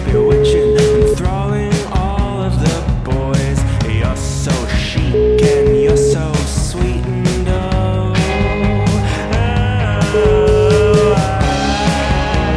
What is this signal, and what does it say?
Ringback tone (Europe, 425 Hz), with noise.